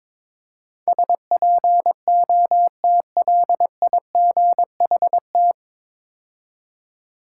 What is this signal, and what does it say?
Morse code, heard as a keyed tone, about 22 wpm, 700 Hz. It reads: SPOTLIGHT
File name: 9 Letter Words - First 3 Letters Given - 22wpm